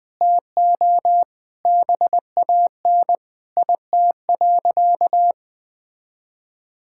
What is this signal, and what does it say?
Morse code, heard as a keyed tone, about 20 wpm, 700 Hz. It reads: TO BAN IT.
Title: Sentences with 7-Word Phrases - Spoken Beginning - 20wpm